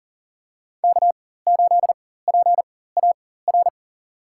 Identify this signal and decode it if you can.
Morse code, heard as a keyed tone, about 40 wpm, spaced as for 19 wpm, 700 Hz. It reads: K8PAR
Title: USA Call Signs - 4x Character Spacing - 40wpm